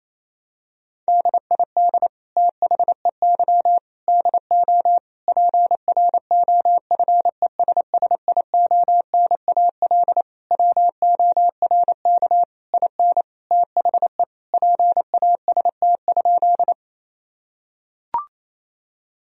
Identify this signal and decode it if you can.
Morse code, heard as a keyed tone, about 28 wpm, 700 Hz. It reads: DID THEY DO PROFESSIONAL WORK IN THE PAST?
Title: Sentences from Top 1000 Words - Review Part 4 - 28wpm